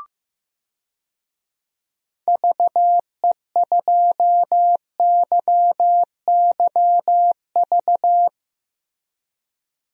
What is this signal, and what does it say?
Morse code, heard as a keyed tone, about 15 wpm, 700 Hz. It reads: VE2YYV
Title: DXCC-100 Call Signs - 6 Characters - Repeated 2x - 15wpm